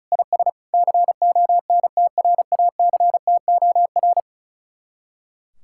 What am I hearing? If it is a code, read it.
Morse code, heard as a keyed tone, about 46 wpm, 700 Hz. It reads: IS CONTRACTOR